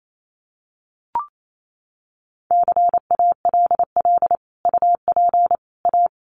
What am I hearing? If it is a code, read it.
Morse code, heard as a keyed tone, about 28 wpm, 700 Hz. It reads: CALL UP A
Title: Sentences from Top 1500 Words 28wpm